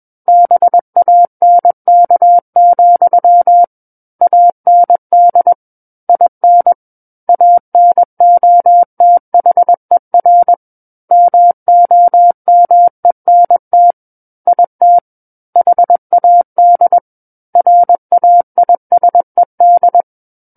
Morse code, heard as a keyed tone, about 21 wpm, 700 Hz. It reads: BANK, AND IN ANOTHER MOMENT IT HAD RAISED
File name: Book One / 12-Chapter 12 - 21 WPM